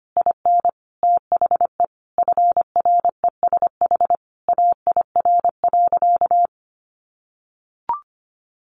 Morse code, heard as a keyed tone, about 25 wpm, 700 Hz. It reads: IN THE FRESH AIR.
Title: Sentences from Top 1200 Words 25wpm